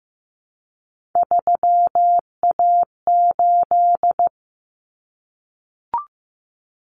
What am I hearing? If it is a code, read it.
Morse code, heard as a keyed tone, about 15 wpm, 700 Hz. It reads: 3A8
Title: Common Cut Numbers - 3-Digit 15wpm